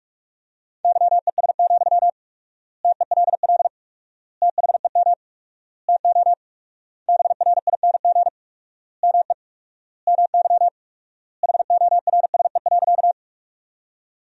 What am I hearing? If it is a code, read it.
Morse code, heard as a keyed tone, about 46 wpm, 700 Hz. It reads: YES, TELL THEM TO BRING ME MY HORSE.
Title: 4x Word-Spacing - Sentences from Top 500 Words - 45wpm